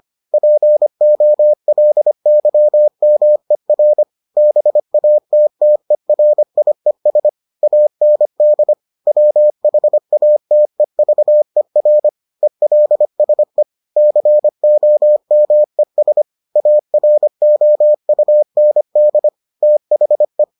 Morse code, heard as a keyed tone, about 25 wpm, 600 Hz. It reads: POLYMER BATTERIES AND WHATEVER ELSE COMES AROUND THE